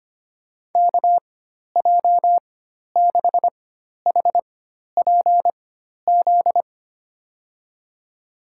Morse code, heard as a keyed tone, about 25 wpm, spaced as for 12 wpm, 700 Hz. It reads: KJ6HPZ
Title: USA Call Signs - 6 Characters - 4x Character Spacing - 25wpm